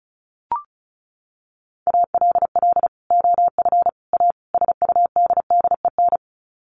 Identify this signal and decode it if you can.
Morse code, heard as a keyed tone, about 34 wpm, 700 Hz. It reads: ALL OF A SUDDEN
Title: Sets of 4 Words 35wpm